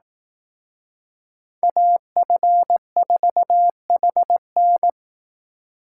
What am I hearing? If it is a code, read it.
Morse code, heard as a keyed tone, about 18 wpm, 700 Hz. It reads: AF4HN
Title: Speed Racing - USA 2x2 Call Signs 15wpm